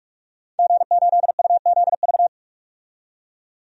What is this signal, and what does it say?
Morse code, heard as a keyed tone, about 41 wpm, 700 Hz. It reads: G8UZV
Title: DXCC-100 Call Signs - 5 Characters - 45wpm